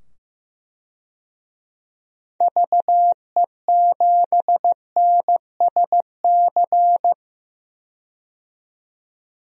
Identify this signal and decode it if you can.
Morse code, heard as a keyed tone, about 15 wpm, 700 Hz. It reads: VE7NSC
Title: DXCC-100 Call Signs - 6 Characters - 15wpm